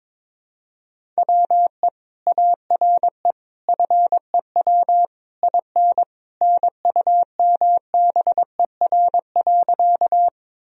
Morse code, heard as a keyed tone, about 22 wpm, 700 Hz. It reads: WE ARE FEW IN NUMBER.
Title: Sentences from Top 1000 Words - Review Part 1 - 22wpm